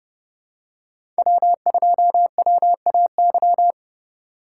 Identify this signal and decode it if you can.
Morse code, heard as a keyed tone, about 30 wpm, 700 Hz. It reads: W2WAY
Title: USA Call Signs - 4x Character Spacing - 30wpm